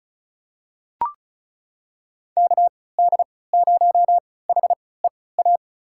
Morse code, heard as a keyed tone, about 34 wpm, spaced as for 20 wpm, 700 Hz. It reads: KD0HEA
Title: USA Call Signs - 6 Characters - 3x Character Spacing - 35wpm